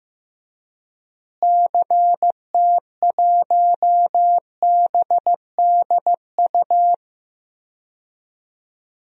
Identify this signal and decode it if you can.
Morse code, heard as a keyed tone, about 15 wpm, 700 Hz. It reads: CT1BDU
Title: DXCC-100 Call Signs - 6 Characters - Repeated 2x - 15wpm